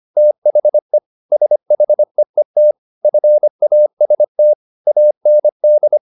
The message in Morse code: THE SHEET FAST AND

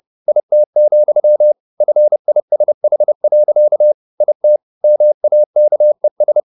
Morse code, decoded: IT, FISH. IT MAKES

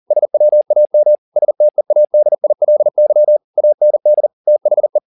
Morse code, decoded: SWAM STEADILY AND THE

